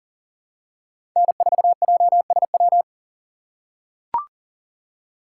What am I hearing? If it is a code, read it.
Morse code, heard as a keyed tone, about 40 wpm, 700 Hz. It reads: N4JSW
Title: USA Call Signs - 2x Character Spacing - 40wpm